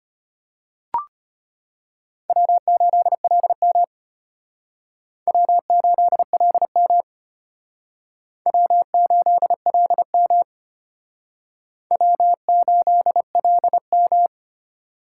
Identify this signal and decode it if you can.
Morse code, heard as a keyed tone, about 32 wpm, 700 Hz. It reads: W8LM W8LM W8LM W8LM